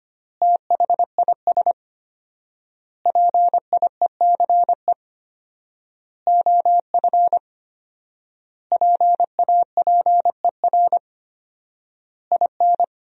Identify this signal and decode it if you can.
Morse code, heard as a keyed tone, about 25 wpm, 700 Hz. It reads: THIS PIECE OF PAPER IN
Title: 4x Word-Spacing - Sentences from Top 400 Words - 25wpm